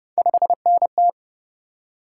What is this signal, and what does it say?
Morse code, heard as a keyed tone, about 28 wpm, 700 Hz. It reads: 5NT